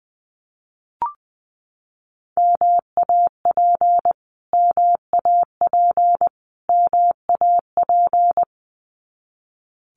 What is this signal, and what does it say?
Morse code, heard as a keyed tone, about 20 wpm, 700 Hz. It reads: MAP MAP MAP
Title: Top 500 Words Repeated 20wpm